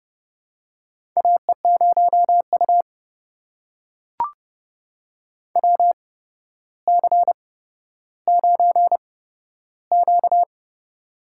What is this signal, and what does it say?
Morse code, heard as a keyed tone, about 30 wpm, spaced as for 13 wpm, 700 Hz. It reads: AE0U WC9Q
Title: USA Call Signs - 4 Characters - 8x Character Spacing - 30wpm